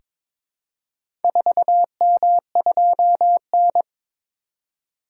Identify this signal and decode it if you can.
Morse code, heard as a keyed tone, about 22 wpm, 700 Hz. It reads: 4M2N
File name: practice / 22wpm / Four Letter-Number Combinations - 22wpm